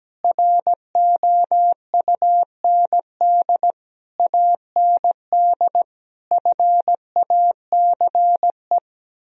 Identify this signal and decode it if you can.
Morse code, heard as a keyed tone, about 17 wpm, 700 Hz. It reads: ROUND AND FACE
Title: Sentences from Top 1000 Words - Review Part 2 - 17wpm